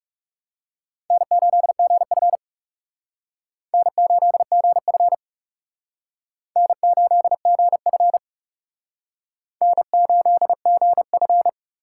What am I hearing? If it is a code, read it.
Morse code, heard as a keyed tone, about 37 wpm, 700 Hz. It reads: N8GF N8GF N8GF N8GF
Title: Speed Racing - USA 1x2 Call Signs 30wpm